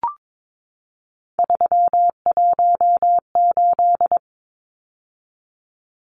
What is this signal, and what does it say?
Morse code, heard as a keyed tone, about 22 wpm, 700 Hz. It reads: 318
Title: Numbers - 3-Digit 22wpm